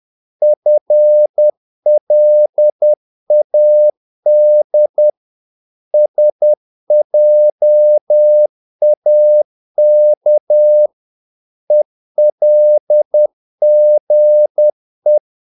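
Morse code, heard as a keyed tone, about 10 wpm, 600 Hz. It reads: FLAD SJAK ELGE
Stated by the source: Morse practice of short Danish words at 10 wpm, from CW med Gnister.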